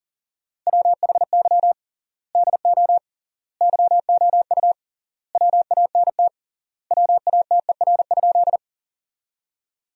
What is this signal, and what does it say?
Morse code, heard as a keyed tone, about 37 wpm, 700 Hz. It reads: WHY DO YOU WANT WATER?
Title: 3x Word-Spacing - Sentences from Top 200 Words - 40wpm